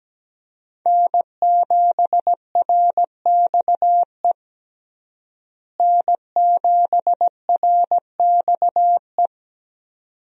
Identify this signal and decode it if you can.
Morse code, heard as a keyed tone, about 17 wpm, 700 Hz. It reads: N7RXE N7RXE
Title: USA Call Signs - 5 Characters - Repeated 2x - 17wpm